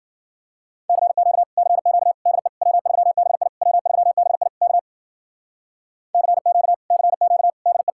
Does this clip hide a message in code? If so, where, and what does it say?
Yes: Morse code from 0.9 to 7.9 s, heard as a keyed tone, about 52 wpm, 700 Hz. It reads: CQ CQ DE W3BI W3BI K CQ CQ DE